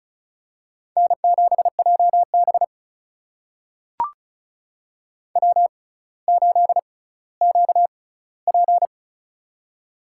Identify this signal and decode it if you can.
Morse code, heard as a keyed tone, about 36 wpm, spaced as for 18 wpm, 700 Hz. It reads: N7JB W8QP
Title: USA Call Signs - 4 Characters - 6x Character Spacing - 35wpm